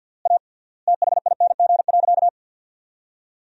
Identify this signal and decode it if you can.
Morse code, heard as a keyed tone, about 52 wpm, 700 Hz. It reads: A THING.
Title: 3x Word-Spacing - Sentences from Top 200 Words - 50wpm